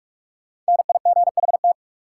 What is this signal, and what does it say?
Morse code, heard as a keyed tone, about 46 wpm, 700 Hz. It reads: NIGHT
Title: Sets of 4 Words - New Word Drill - 45wpm